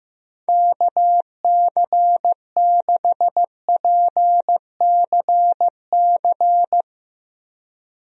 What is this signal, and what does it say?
Morse code, heard as a keyed tone, about 15 wpm, 700 Hz. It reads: KC6PCC
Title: USA Call Signs - 6 Characters - Repeated 3x - 15wpm